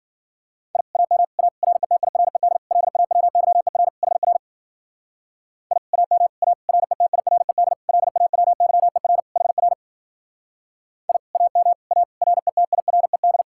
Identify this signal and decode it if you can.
Morse code, heard as a keyed tone, about 50 wpm, 700 Hz. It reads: I AM A RETIRED LAWYER HR I AM A RETIRED LAWYER HR I AM A RETIRED
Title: Speed-Racing Common QSO Phrases Encore - 40wpm